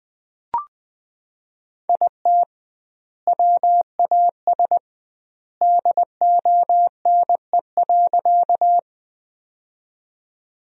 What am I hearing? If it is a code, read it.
Morse code, heard as a keyed tone, about 20 wpm, 700 Hz. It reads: IT WAS DONE.
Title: 2x Word-Spacing - Sentences from Top 500 Words - 20wpm